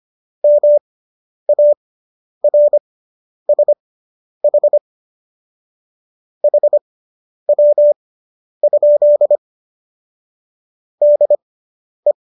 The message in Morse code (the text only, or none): MARSH HW? DE